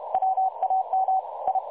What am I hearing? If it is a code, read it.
Morse code, heard as a keyed tone, about 44 wpm, spaced as for 36 wpm, 750 Hz. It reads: WAM I
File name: dlazaawansowanych / nagraniazpasma / websdr recording 3565.0kHz